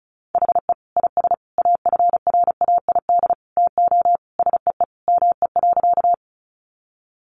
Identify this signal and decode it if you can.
Morse code, heard as a keyed tone, about 34 wpm, 700 Hz. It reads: HE IS AFRAID TO SEE ME.
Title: Sentences from Top 1500 Words 35wpm